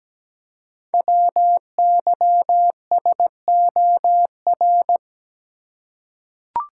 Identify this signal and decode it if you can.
Morse code, heard as a keyed tone, about 17 wpm, 700 Hz. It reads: WYSOR
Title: Code Groups - 5 Letters - 17wpm